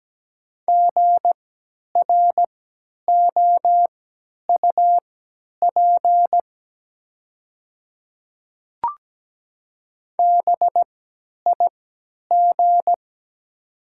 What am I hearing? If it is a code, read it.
Morse code, heard as a keyed tone, about 17 wpm, spaced as for 10 wpm, 700 Hz. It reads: GROUP BIG
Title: 3x Character Spacing - Top 300 Words - 17wpm